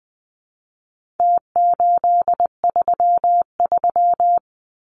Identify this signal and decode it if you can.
Morse code, heard as a keyed tone, about 20 wpm, 700 Hz. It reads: T833
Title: Numbers - Contest Style - 4-Digit 20wpm